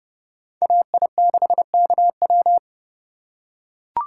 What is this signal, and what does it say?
Morse code, heard as a keyed tone, about 29 wpm, 700 Hz. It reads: AI6KW